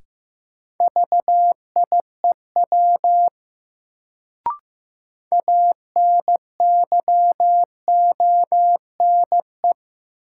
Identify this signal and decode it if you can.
Morse code, heard as a keyed tone, about 15 wpm, 700 Hz. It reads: VIEW ANYONE
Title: Sentences from Top 900 Words - New Word Drill - 15wpm